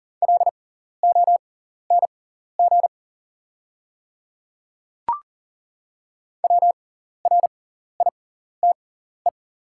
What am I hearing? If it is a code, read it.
Morse code, heard as a keyed tone, about 37 wpm, spaced as for 14 wpm, 690 Hz. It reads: LONG WRITE
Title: 6x Character Spacing - Top 100 Words - 40wpm